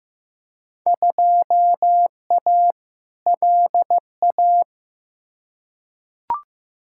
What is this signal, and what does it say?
Morse code, heard as a keyed tone, about 15 wpm, 700 Hz. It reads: 2A LA
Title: Field Day Exchanges - 15wpm